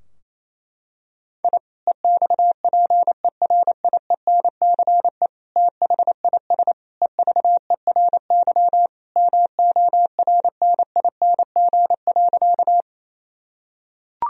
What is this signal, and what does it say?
Morse code, heard as a keyed tone, about 28 wpm, 700 Hz. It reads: I EXPERIENCE THIS EVERY MORNING.
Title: Sentences from Top 1000 Words - Review Part 3 - 28wpm